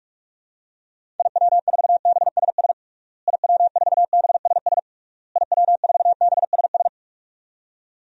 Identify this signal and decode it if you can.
Morse code, heard as a keyed tone, about 46 wpm, 700 Hz. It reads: IW4BSS IW4BSS IW4BSS
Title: DXCC-100 Call Signs - 6 Characters - Repeated 3x - 45wpm